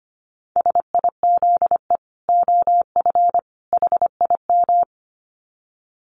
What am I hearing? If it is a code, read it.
Morse code, heard as a keyed tone, about 25 wpm, 700 Hz. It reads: SIZE OF HIM
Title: Dit Challenge 25wpm